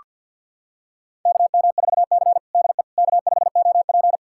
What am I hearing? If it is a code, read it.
Morse code, heard as a keyed tone, about 46 wpm, 700 Hz. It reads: KM4C DE K5OP
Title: USA Callsign DE Callsign - 4 Characters - 50wpm